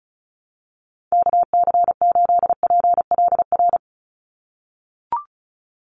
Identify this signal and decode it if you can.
Morse code, heard as a keyed tone, about 34 wpm, 700 Hz. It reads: KC8PLR
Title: USA Call Signs 35wpm